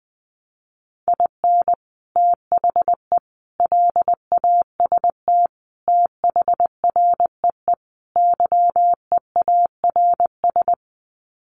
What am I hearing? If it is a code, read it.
Morse code, heard as a keyed tone, about 20 wpm, 700 Hz. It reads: IN THE LAST THREE YEARS
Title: Sets of 5 Words Encore2 20wpm